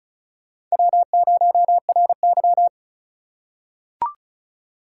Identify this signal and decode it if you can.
Morse code, heard as a keyed tone, about 36 wpm, 700 Hz. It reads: W0RY
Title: USA Call Signs - 4 Characters - 4x Character Spacing - 35wpm